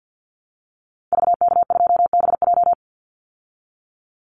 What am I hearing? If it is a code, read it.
Morse code, heard as a keyed tone, about 50 wpm, 700 Hz. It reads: VK2BJ
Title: Call Signs Repeated 2x - 50wpm